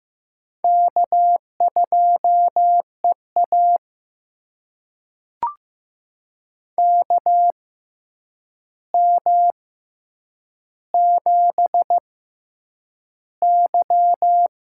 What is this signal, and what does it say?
Morse code, heard as a keyed tone, about 15 wpm, spaced as for 8 wpm, 700 Hz. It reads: K2EA KM7Y
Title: USA Call Signs - 4 Characters - 6x Character Spacing - 15wpm